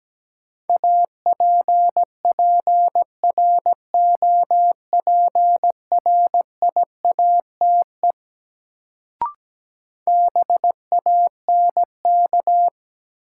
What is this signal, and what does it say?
Morse code, heard as a keyed tone, about 17 wpm, 700 Hz. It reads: APPROPRIATE BANK
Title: Sentences from Top 900 Words - New Word Drill - 17wpm